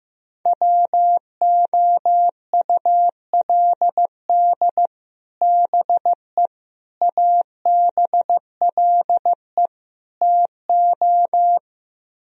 Morse code, heard as a keyed tone, about 15 wpm, 700 Hz. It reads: WOULD BE ABLE TO
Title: Speed-Racing - Sets of 4 Words 15wpm